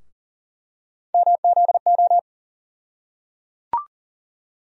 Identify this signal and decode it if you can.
Morse code, heard as a keyed tone, about 37 wpm, 700 Hz. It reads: MZO
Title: USA Call Sign Postfixes - 3 Letters 40wpm